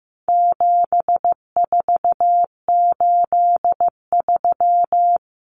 Morse code, heard as a keyed tone, about 15 wpm, 700 Hz. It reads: 7483
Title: Common Cut Numbers - 4-Digit 15wpm